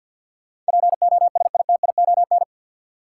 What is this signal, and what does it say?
Morse code, heard as a keyed tone, about 49 wpm, 700 Hz. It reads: POSITION
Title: Sentences from Top 900 Words - New Word Drill - 50wpm